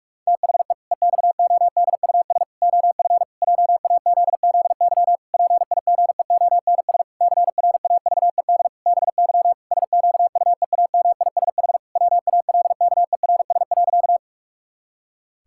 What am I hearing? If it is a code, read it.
Morse code, heard as a keyed tone, about 41 wpm, 700 Hz. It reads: THE EXODUS OF JAZZY PIGEONS CRAVED BY SQUEAMISH WALKERS.
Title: Pangram Challenge v2 - 45wpm